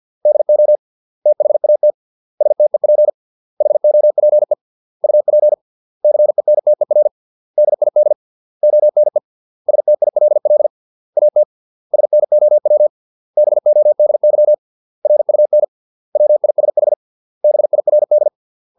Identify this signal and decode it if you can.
Morse code, heard as a keyed tone, about 50 wpm, 600 Hz. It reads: DO THAT STEP HOPE UP CENTER DID ONE STILL AT SNOW BODY RUN WISH BIRD